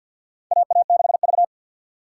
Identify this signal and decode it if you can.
Morse code, heard as a keyed tone, about 52 wpm, 700 Hz. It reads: AA6V
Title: USA Call Signs - 4 Characters - 2x Character Spacing - 50wpm